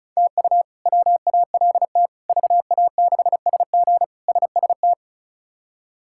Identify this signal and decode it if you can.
Morse code, heard as a keyed tone, about 34 wpm, 700 Hz. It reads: TU WALT VA6SG SST
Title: K1USN SST QSO Practice - 35wpm